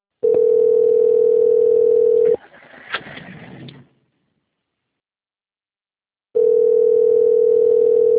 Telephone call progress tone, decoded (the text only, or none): ringback tone